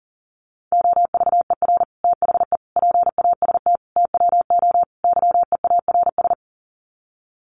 Morse code, heard as a keyed tone, about 40 wpm, 700 Hz. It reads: OVER THE PAST TWO YEARS
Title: Sets of 5 Words Encore 40wpm